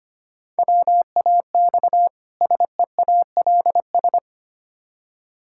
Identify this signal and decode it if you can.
Morse code, heard as a keyed tone, about 25 wpm, 700 Hz. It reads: WAX SEALS
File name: practice / 25wpm / Difficult Words with Hints - 25wpm